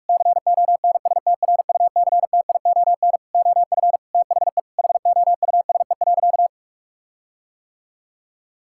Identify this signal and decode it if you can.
Morse code, heard as a keyed tone, about 41 wpm, 700 Hz. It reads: CONSTRUCTION OF THE HOUSE.
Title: Sentences from Top 1600 Words 45wpm